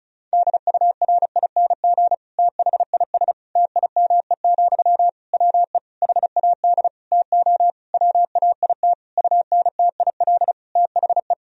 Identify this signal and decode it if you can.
Morse code, heard as a keyed tone, about 36 wpm, 700 Hz. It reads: DURING THIS TIME, WE HAD TO WAIT UNTIL THE
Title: Sentences from Top 1000 Words - Review Part 2 - 35wpm